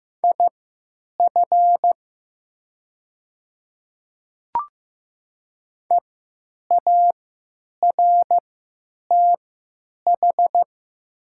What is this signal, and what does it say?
Morse code, heard as a keyed tone, about 15 wpm, spaced as for 9 wpm, 700 Hz. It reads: IF EARTH
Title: 3x Character Spacing - Top 200 Words - 15wpm